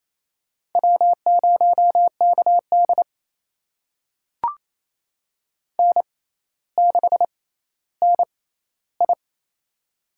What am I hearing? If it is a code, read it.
Morse code, heard as a keyed tone, about 28 wpm, spaced as for 14 wpm, 700 Hz. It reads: W0KD N6NI
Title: USA Call Signs - 4 Characters - 6x Character Spacing - 28wpm